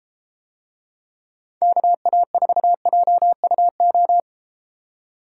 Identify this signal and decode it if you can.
Morse code, heard as a keyed tone, about 33 wpm, 700 Hz. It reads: KA4JUO